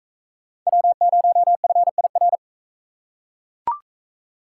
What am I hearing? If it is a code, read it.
Morse code, heard as a keyed tone, about 40 wpm, 700 Hz. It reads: W0FIR